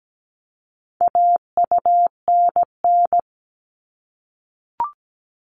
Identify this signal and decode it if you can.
Morse code, heard as a keyed tone, about 17 wpm, 700 Hz. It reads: AUNN
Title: All Cut Numbers - 4-Digit 17wpm